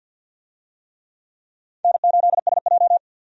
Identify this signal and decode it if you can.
Morse code, heard as a keyed tone, about 51 wpm, 700 Hz. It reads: N8SJ
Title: USA Call Signs - 4 Characters - 50wpm